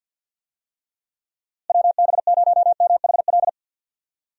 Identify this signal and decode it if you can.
Morse code, heard as a keyed tone, about 48 wpm, 700 Hz. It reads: WB0MHL
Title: USA Call Signs - 6 Characters - Repeated 2x - 50wpm